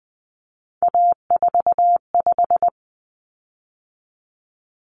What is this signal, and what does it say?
Morse code, heard as a keyed tone, about 20 wpm, 700 Hz. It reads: A45
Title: Common Cut Numbers - 3-Digit 20wpm